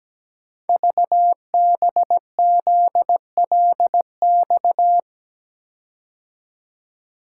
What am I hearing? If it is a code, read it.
Morse code, heard as a keyed tone, about 17 wpm, 700 Hz. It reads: VBZLX
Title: Code Groups - 5 Letters - 17wpm